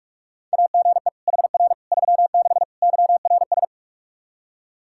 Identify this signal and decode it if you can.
Morse code, heard as a keyed tone, about 46 wpm, 700 Hz. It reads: AGE HR 36 YRS
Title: Practice QSO - Exchange 3 - More Info - 45wpm